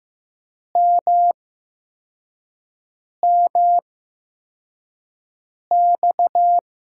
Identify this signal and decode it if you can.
Morse code, heard as a keyed tone, about 15 wpm, spaced as for 4 wpm, 700 Hz. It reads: MMX
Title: USA Call Signs - 6 Characters - 8x Character Spacing - 15wpm